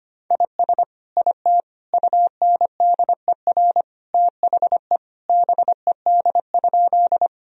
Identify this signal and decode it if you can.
Morse code, heard as a keyed tone, about 25 wpm, 700 Hz. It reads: IS IT UNDER THE BED?
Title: Sentences from Top 1000 Words - Review Part 1 - 25wpm